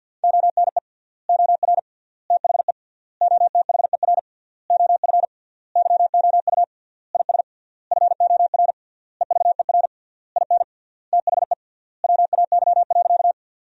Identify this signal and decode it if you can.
Morse code, heard as a keyed tone, about 50 wpm, 700 Hz. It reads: ONE OR THE OTHER OF YOU IS FOR EVER IN THE WAY.
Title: 3x Word-Spacing - Sentences from Top 400 Words - 50wpm